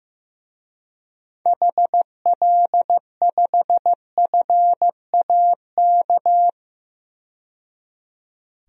Morse code, heard as a keyed tone, about 15 wpm, 700 Hz. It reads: HL5FAK
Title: DXCC-100 Call Signs - 6 Characters - 15wpm